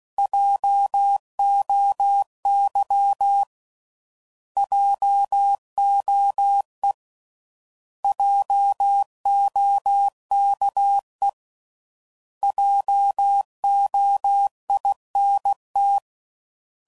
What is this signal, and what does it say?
Morse code, heard as a keyed tone, about 16 wpm, 800 Hz. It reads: JOY JOE JOKE JOINT